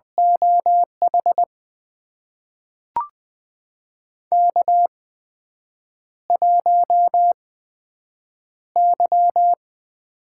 Morse code, heard as a keyed tone, about 20 wpm, spaced as for 7 wpm, 700 Hz. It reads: OH K1Y
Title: USA Call Signs - 4 Characters - 8x Character Spacing - 20wpm